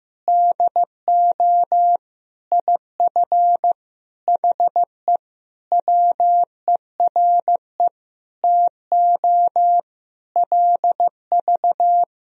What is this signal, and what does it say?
Morse code, heard as a keyed tone, about 15 wpm, 700 Hz. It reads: DO IF HE WERE TO LV
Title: Sentences from Top 400 Words and Abbreviations 15wpm